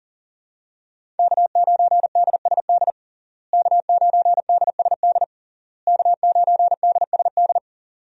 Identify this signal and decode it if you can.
Morse code, heard as a keyed tone, about 37 wpm, 700 Hz. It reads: K9DSD K9DSD K9DSD
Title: USA Call Signs - 5 Characters - Repeated 3x - 40wpm